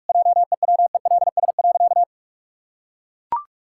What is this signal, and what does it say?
Morse code, heard as a keyed tone, about 41 wpm, 700 Hz. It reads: JEWELS.
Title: Pangram Challenge v2 - 45wpm